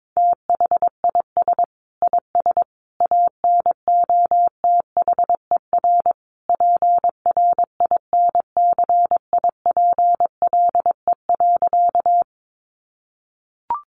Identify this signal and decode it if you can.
Morse code, heard as a keyed tone, about 22 wpm, 700 Hz. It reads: THIS IS ANOTHER PRINCIPLE.
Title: Sentences from Top 1400 Words 22wpm